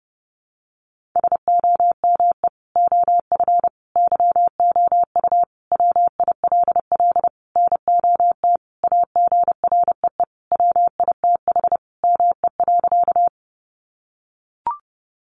Morse code, heard as a keyed tone, about 30 wpm, 700 Hz. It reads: SOME OF YOU WILL NOT AGREE WITH ME.
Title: Sentences from Top 1100 Words 30wpm